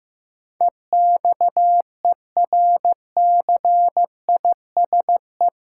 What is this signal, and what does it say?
Morse code, heard as a keyed tone, about 15 wpm, 700 Hz. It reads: EXERCISE
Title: Sentences from Top 1200 Words - New Word Drill Alt - 15wpm